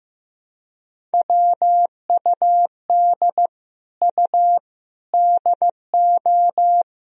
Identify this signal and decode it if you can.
Morse code, heard as a keyed tone, about 15 wpm, 700 Hz. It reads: WUD U DO
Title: Sentences from Top 100 Words and Abbreviations 15wpm